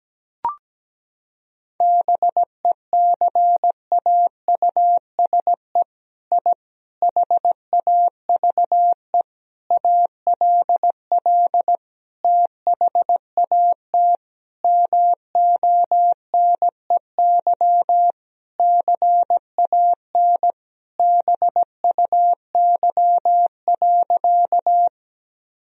Morse code, heard as a keyed tone, about 17 wpm, 700 Hz. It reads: BECAUSE I HAVE ALL THAT MONEY CAN BUY.
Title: Sentences from Top 1000 Words - Review Part 4 - 17wpm